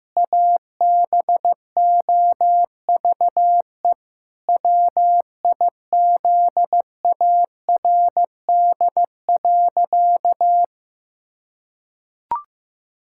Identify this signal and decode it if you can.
Morse code, heard as a keyed tone, about 15 wpm, 700 Hz. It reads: ABOVE WIZARD.
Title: Pangram Challenge v2 - 15wpm